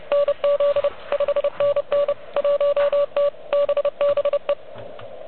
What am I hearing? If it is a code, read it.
Morse code, heard as a keyed tone, about 27 wpm, 590 Hz. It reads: NZ 5NN 1T BBE
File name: OK QRP 2022 / Zvuk / REC36